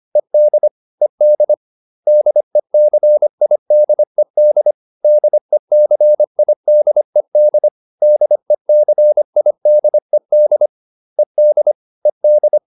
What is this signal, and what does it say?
Morse code, heard as a keyed tone, about 25 wpm, 600 Hz. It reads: ED ED DECIDED DECIDED DECIDED ED ED